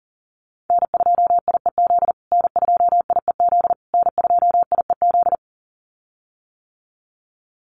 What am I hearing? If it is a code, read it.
Morse code, heard as a keyed tone, about 40 wpm, 700 Hz. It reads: N2IEZ N2IEZ N2IEZ
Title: USA Call Signs Repeated 3x 40wpm